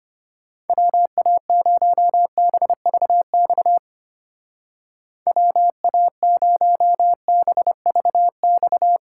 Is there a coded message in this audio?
Yes: Morse code from 0.7 to 9.0 s, heard as a keyed tone, about 32 wpm, 700 Hz. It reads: WA0BVX WA0BVX